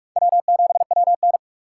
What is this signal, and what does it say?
Morse code, heard as a keyed tone, about 40 wpm, 700 Hz. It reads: W7WN